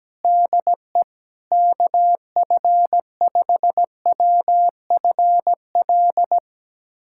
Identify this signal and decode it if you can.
Morse code, heard as a keyed tone, about 17 wpm, 700 Hz. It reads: DE KF5WFL
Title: USA Callsign DE Callsign 17wpm